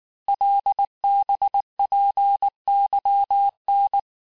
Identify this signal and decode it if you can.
Morse code, heard as a keyed tone, about 19 wpm, 800 Hz. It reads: LBPYN
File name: cours-du-lundi files / cours-1589-texte-chiffre-vitesse-18mpm